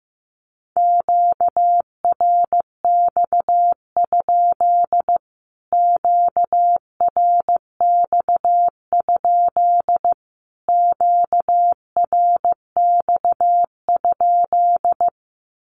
Morse code, heard as a keyed tone, about 15 wpm, 700 Hz. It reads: QRX? QRX? QRX?
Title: Q-Codes Repeated 3x 15wpm